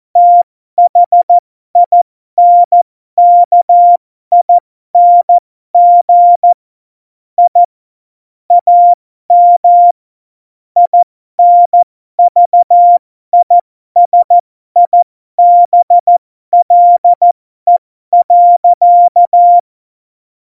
Morse code, heard as a keyed tone, about 14 wpm, spaced as for 12 wpm, 700 Hz. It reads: THINKING I AM INVISIBLE.
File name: Morsekurs / Witze